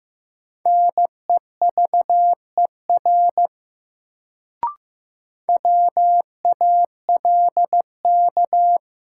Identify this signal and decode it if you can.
Morse code, heard as a keyed tone, about 16 wpm, 700 Hz. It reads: NEVER WALK
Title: Sentences from Top 300 Words - New Word Drill - 15wpm